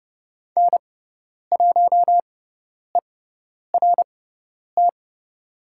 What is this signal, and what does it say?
Morse code, heard as a keyed tone, about 30 wpm, spaced as for 10 wpm, 690 Hz. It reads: N1ERT
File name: Lesson 17g - Letter u - Call Signs 30wpm